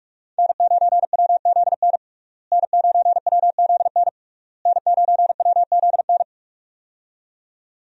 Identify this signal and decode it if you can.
Morse code, heard as a keyed tone, about 42 wpm, 700 Hz. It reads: N9WZN N9WZN N9WZN
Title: USA Call Signs - 5 Characters - Repeated 3x - 45wpm